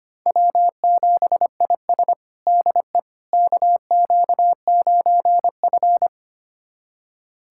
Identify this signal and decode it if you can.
Morse code, heard as a keyed tone, about 25 wpm, 700 Hz. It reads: W7IS DE KQ9F
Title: USA Callsign DE Callsign - 4 Characters - 25wpm